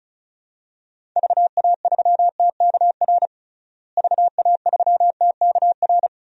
Morse code, heard as a keyed tone, about 36 wpm, 700 Hz. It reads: VA3TKR VA3TKR
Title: DXCC-100 Call Signs - 6 Characters - Repeated 2x - 35wpm